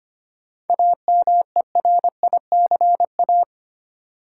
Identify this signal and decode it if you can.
Morse code, heard as a keyed tone, about 25 wpm, 700 Hz. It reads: AMERICA
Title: Sets of 5 Words - New Word Drill - 25wpm